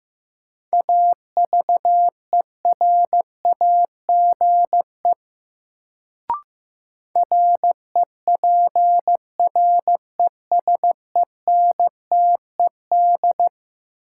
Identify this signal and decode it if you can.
Morse code, heard as a keyed tone, about 15 wpm, 700 Hz. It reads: AVERAGE REPRESENTED
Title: Sets of 5 Words - New Word Drill - 15wpm